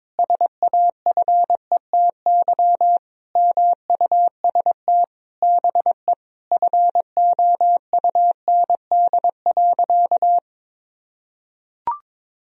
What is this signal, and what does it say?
Morse code, heard as a keyed tone, about 22 wpm, 700 Hz. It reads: SAFETY MUST BE FOUND.
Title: Sentences from Top 1000 Words - Review Part 4 - 22wpm